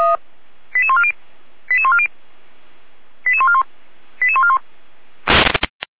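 Five-tone selective call (ZVEI), heard as quick runs of five tones, tones of about 70 ms. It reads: 88140 88140 88141 88141
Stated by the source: ZVEI five-tone selective call received on SDR, channel 455, code 88501, from audio.f85360.feuerwehr-orlamuende.org fire brigade server.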